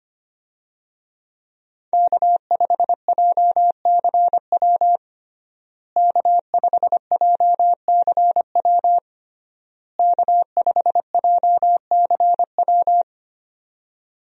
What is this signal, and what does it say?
Morse code, heard as a keyed tone, about 25 wpm, 700 Hz. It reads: K5JCW K5JCW K5JCW
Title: USA Call Signs - 5 Characters - Repeated 3x - 25wpm